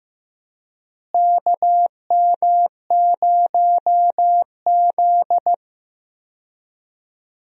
Morse code, heard as a keyed tone, about 15 wpm, 700 Hz. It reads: KM0Z
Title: USA Call Signs - 4 Characters - Repeated 3x - 15wpm